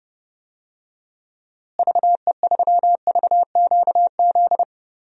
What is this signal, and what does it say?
Morse code, heard as a keyed tone, about 32 wpm, 700 Hz. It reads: VE3VQZ
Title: DXCC-100 Call Signs - 6 Characters - Repeated 2x - 30wpm